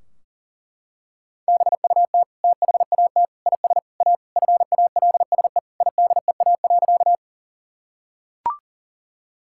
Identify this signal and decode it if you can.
Morse code, heard as a keyed tone, about 40 wpm, 700 Hz. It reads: BUT THAT IS A FALSE IDEA.
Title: Sentences from Top 1600 Words 40wpm